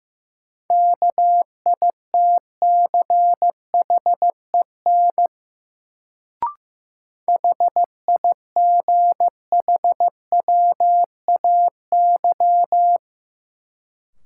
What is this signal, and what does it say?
Morse code, heard as a keyed tone, about 15 wpm, 700 Hz. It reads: KITCHEN HIGHWAY
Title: Sentences from Top 1400 Words - New Word Drill Alt - 15wpm